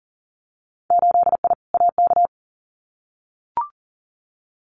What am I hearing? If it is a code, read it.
Morse code, heard as a keyed tone, about 42 wpm, 700 Hz. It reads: 8I AK
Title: Winter Field Day Exchanges Encore 40wpm